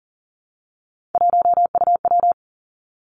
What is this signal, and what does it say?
Morse code, heard as a keyed tone, about 40 wpm, 700 Hz. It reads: 1UW
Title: Three Challenging Letter-Number Combinations 40wpm